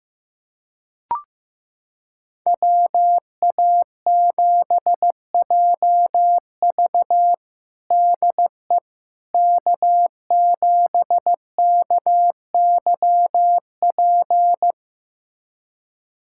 Morse code, heard as a keyed tone, about 15 wpm, 700 Hz. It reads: WA7JV DE K7KYP
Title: USA Callsign DE Callsign - 5 Characters - 15wpm